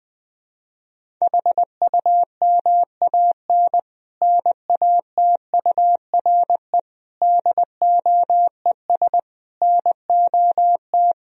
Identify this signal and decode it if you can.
Morse code, heard as a keyed tone, about 20 wpm, 700 Hz. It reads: HUMAN NATURE DOES NOT
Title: Sentences from Top 1000 Words - Review Part 3 - 20wpm